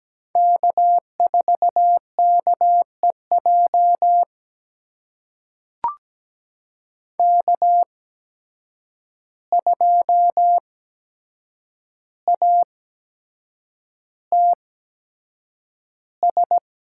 Morse code, heard as a keyed tone, about 17 wpm, spaced as for 7 wpm, 700 Hz. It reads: K4KEJ K2ATS
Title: USA Call Signs - 5 Characters - 8x Character Spacing - 17wpm